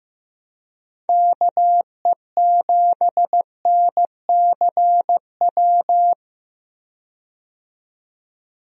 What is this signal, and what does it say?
Morse code, heard as a keyed tone, about 15 wpm, 700 Hz. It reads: KE7NCW
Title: USA Call Signs - 6 Characters - 15wpm